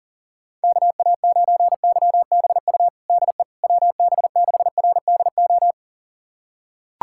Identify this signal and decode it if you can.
Morse code, heard as a keyed tone, about 40 wpm, 700 Hz. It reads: KA9YBU DE WB6RDO
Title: USA Callsign DE Callsign 40wpm